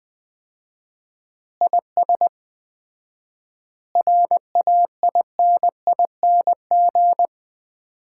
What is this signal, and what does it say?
Morse code, heard as a keyed tone, about 20 wpm, 700 Hz. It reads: IS RAINING
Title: Common QSO Phrases - 4x Word-Spacing - 20wpm